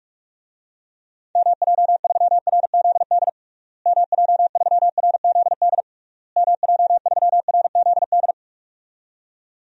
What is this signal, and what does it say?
Morse code, heard as a keyed tone, about 44 wpm, 700 Hz. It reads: MJ3RZD MJ3RZD MJ3RZD
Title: DXCC-100 Call Signs - 6 Characters - Repeated 3x - 45wpm